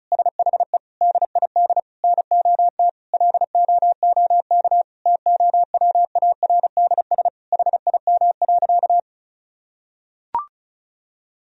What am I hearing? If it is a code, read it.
Morse code, heard as a keyed tone, about 36 wpm, 700 Hz. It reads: SHE DID NOT LOOK TOWARDS HIM.
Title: Sentences from Top 1000 Words - Review Part 2 - 35wpm